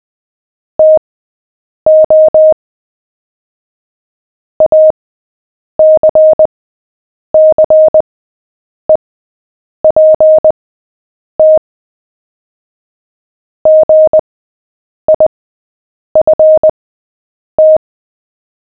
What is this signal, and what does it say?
Morse code, heard as a keyed tone, about 20 wpm, spaced as for 8 wpm, 620 Hz. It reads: TO ACCEPT GIFT
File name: cw / fists / Short-Story-7-8wpm